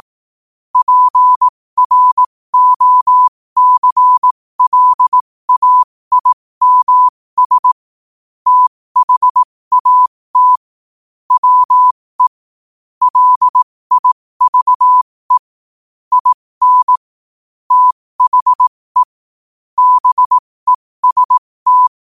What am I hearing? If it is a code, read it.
Morse code, heard as a keyed tone, about 18 wpm, spaced as for 15 wpm, 1000 Hz. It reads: PROCLAIMS THAT WE LIVE IN THE BEST
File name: qotdfiles / 15wpm